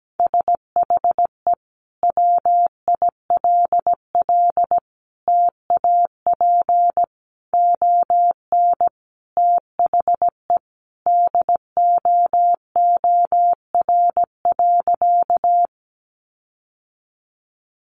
Morse code, heard as a keyed tone, about 17 wpm, 700 Hz. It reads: SHE WILL TAP ON THE DOOR.
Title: Sentences from Top 1400 Words 17wpm